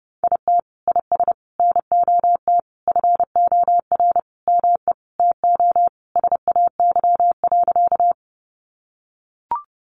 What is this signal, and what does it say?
Morse code, heard as a keyed tone, about 30 wpm, 700 Hz. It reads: IT IS NOT FOR ME TO SAY.
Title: Sentences from Top 600 Words 30wpm